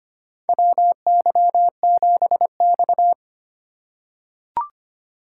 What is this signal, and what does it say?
Morse code, heard as a keyed tone, about 25 wpm, 700 Hz. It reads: WY7X
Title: USA Call Signs - 4 Characters - 4x Character Spacing - 25wpm